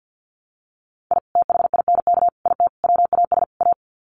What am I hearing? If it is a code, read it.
Morse code, heard as a keyed tone, about 50 wpm, 700 Hz. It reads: I THINK IT WAS A
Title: Sets of 5 Words 50wpm